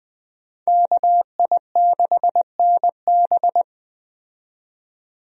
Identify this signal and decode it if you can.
Morse code, heard as a keyed tone, about 20 wpm, 700 Hz. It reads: KI6NB
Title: USA Call Signs - 5 Characters - 20wpm